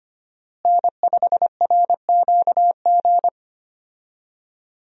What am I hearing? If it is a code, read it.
Morse code, heard as a keyed tone, about 25 wpm, 700 Hz. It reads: N5RQG
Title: USA Call Signs - 5 Characters - Repeated 3x - 25wpm